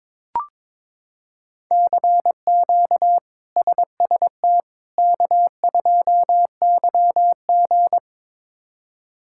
Morse code, heard as a keyed tone, about 22 wpm, 700 Hz. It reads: CQ SST K2YG
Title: K1USN SST QSO Practice - 22wpm